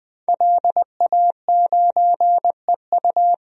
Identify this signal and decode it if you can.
Morse code, heard as a keyed tone, about 20 wpm, 700 Hz. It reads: LA9EU
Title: DXCC-100 Call Signs - 5 Characters - Repeated 2x - 20wpm